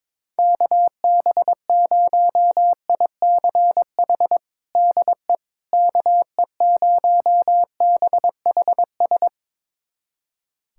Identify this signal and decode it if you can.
Morse code, heard as a keyed tone, about 21 wpm, 700 Hz. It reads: KB0ICH DE KE0BHS
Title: USA Callsign DE Callsign - 6 Characters - 22wpm